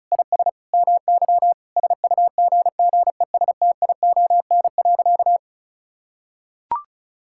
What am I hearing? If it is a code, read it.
Morse code, heard as a keyed tone, about 34 wpm, 700 Hz. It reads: IS MY SUGGESTION.
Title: Sentences from Top 1600 Words 35wpm